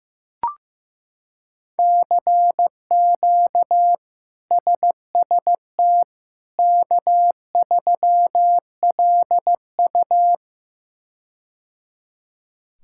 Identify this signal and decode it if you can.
Morse code, heard as a keyed tone, about 15 wpm, 700 Hz. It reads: CQ SST K3LU
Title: K1USN SST QSO Practice - 15wpm